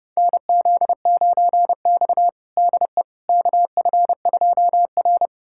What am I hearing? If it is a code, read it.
Morse code, heard as a keyed tone, about 32 wpm, 700 Hz. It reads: NZ9X DE KF2R
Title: USA Callsign DE Callsign - 4 Characters - 30wpm